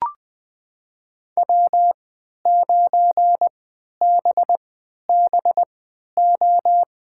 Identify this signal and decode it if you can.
Morse code, heard as a keyed tone, about 20 wpm, spaced as for 11 wpm, 700 Hz. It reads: W9BBO
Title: USA Call Signs - 5 Characters - 3x Character Spacing - 20wpm